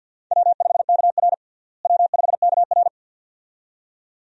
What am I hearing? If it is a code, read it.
Morse code, heard as a keyed tone, about 52 wpm, 700 Hz. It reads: W5KR W5KR
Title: USA Call Signs - 4 Characters - Repeated 2x - 50wpm